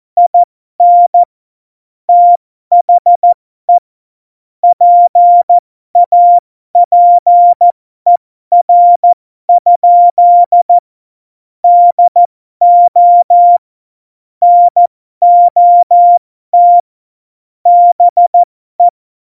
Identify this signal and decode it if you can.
Morse code, heard as a keyed tone, about 14 wpm, spaced as for 12 wpm, 700 Hz. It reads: IN THE PAPER? DO NOT BE
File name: Morsekurs / Witze